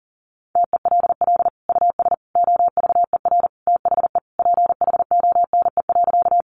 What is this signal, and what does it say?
Morse code, heard as a keyed tone, about 40 wpm, 700 Hz. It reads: TELL US OVER THE PHONE.
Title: Sentences from Top 1100 Words 40wpm